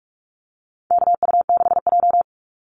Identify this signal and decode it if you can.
Morse code, heard as a keyed tone, about 44 wpm, 700 Hz. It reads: KU6J
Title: NA SOTA Activator Callsigns 45wpm